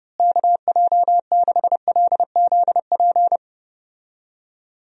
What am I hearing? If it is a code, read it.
Morse code, heard as a keyed tone, about 32 wpm, 700 Hz. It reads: KJ6LZP
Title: USA Call Signs - 6 Characters - Repeated 2x - 30wpm